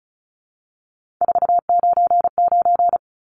Morse code, heard as a keyed tone, about 34 wpm, 700 Hz. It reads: 499
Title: Numbers - 3-Digit 35wpm